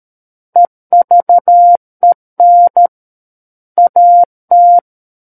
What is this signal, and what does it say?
Morse code, heard as a keyed tone, about 13 wpm, 700 Hz. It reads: EVEN AT
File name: Book One / 04-Chapter 4 - 13 WPM